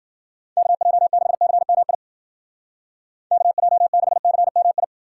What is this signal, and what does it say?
Morse code, heard as a keyed tone, about 49 wpm, 700 Hz. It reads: KJ6CMI KJ6CMI
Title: Speed Racing - USA Call Signs 40wpm